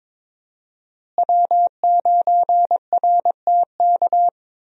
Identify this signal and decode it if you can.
Morse code, heard as a keyed tone, about 22 wpm, 700 Hz. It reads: W9RTK
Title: USA Call Signs - 5 Characters - Repeated 2x - 22wpm